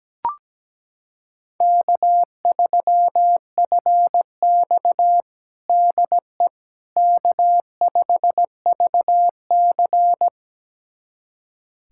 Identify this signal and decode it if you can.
Morse code, heard as a keyed tone, about 17 wpm, 700 Hz. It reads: K3FX DE K5VC
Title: USA Callsign DE Callsign - 4 Characters - 17wpm